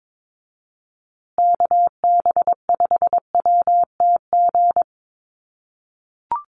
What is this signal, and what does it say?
Morse code, heard as a keyed tone, about 22 wpm, 700 Hz. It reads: KB5WTG
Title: USA Call Signs Repeated 3x 22wpm